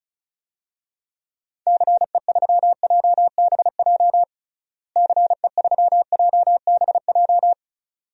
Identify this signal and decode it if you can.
Morse code, heard as a keyed tone, about 34 wpm, 700 Hz. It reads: CE3JBJ CE3JBJ
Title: DXCC-100 Call Signs - 6 Characters - Repeated 2x - 35wpm